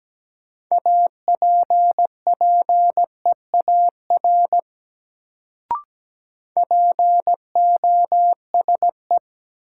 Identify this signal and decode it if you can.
Morse code, heard as a keyed tone, about 17 wpm, 700 Hz. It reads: APPEAR POSE
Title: Sentences from Top 1000 Words - Review Part 2 - New Word Drill - 17wpm